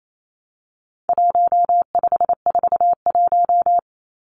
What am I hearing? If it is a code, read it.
Morse code, heard as a keyed tone, about 28 wpm, 700 Hz. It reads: 1541
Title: Numbers - 4-Digit 28wpm